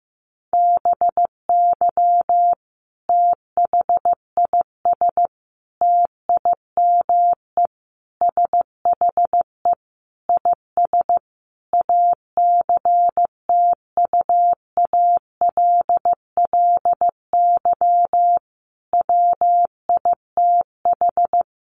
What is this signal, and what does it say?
Morse code, heard as a keyed tone, about 15 wpm, 700 Hz. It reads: BY THIS TIME SHE IS ACTUALLY WITH
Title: Sentences from Top 700 Words 15wpm